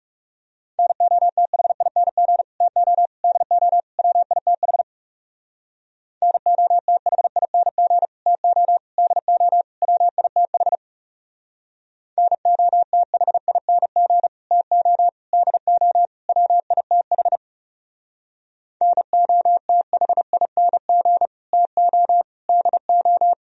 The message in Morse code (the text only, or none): NOTHING TO DO WITH NOTHING TO DO WITH NOTHING TO DO WITH NOTHING TO DO